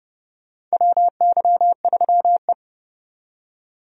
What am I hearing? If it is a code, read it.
Morse code, heard as a keyed tone, about 32 wpm, 700 Hz. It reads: WY3E
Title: USA Call Signs - 4 Characters - 30wpm